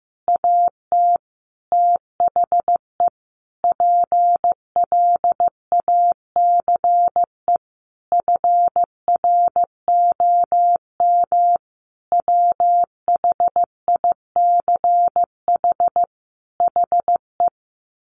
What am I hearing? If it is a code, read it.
Morse code, heard as a keyed tone, about 14 wpm, 700 Hz. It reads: AT THE PLACE FROM WHICH HE
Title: Sentences from Top 1500 Words 15wpm